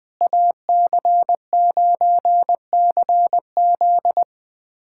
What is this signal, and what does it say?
Morse code, heard as a keyed tone, about 21 wpm, 700 Hz. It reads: AC9CZ
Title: USA Callsign DE Callsign - 5 Characters - 20wpm